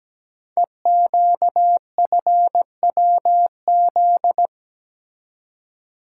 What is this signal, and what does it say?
Morse code, heard as a keyed tone, about 17 wpm, 700 Hz. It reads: EQFWZ
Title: Code Groups - 5 Letter:Numbers - 17wpm